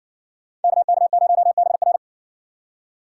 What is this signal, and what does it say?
Morse code, heard as a keyed tone, about 64 wpm, 690 Hz. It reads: KK0BR